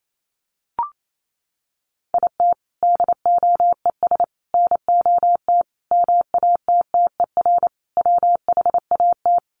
Morse code, heard as a keyed tone, about 28 wpm, 700 Hz. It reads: IT DOES NOT MATTER WHAT
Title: Sets of 5 Words Encore 28wpm